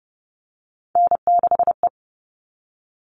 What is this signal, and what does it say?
Morse code, heard as a keyed tone, about 30 wpm, 700 Hz. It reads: N6E
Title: All Cut Numbers - 3-Digit 30wpm